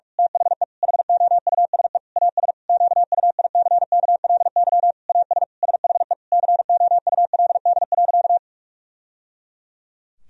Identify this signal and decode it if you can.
Morse code, heard as a keyed tone, about 43 wpm, 700 Hz. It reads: THE HOUSE AS QUICKLY AS SHE COULD.
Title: Sentences from Top 1000 Words - Review Part 3 - 45wpm